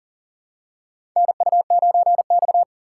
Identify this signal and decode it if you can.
Morse code, heard as a keyed tone, about 40 wpm, 700 Hz. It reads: NU9X
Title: USA Call Signs - 4 Characters - 4x Character Spacing - 40wpm